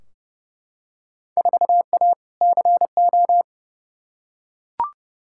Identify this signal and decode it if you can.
Morse code, heard as a keyed tone, about 30 wpm, 700 Hz. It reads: 4A CO
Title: Field Day Exchanges - 30wpm